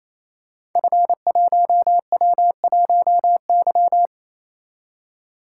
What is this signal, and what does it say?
Morse code, heard as a keyed tone, about 27 wpm, 700 Hz. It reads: F1W1Y
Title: Code Groups - 5 Letter:Numbers - 28wpm